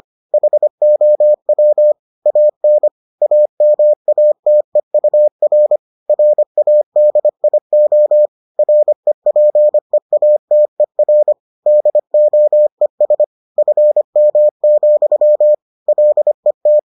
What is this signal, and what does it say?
Morse code, heard as a keyed tone, about 25 wpm, 600 Hz. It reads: HOW AN AMATEUR RADIO REPEATER DOES FM, LET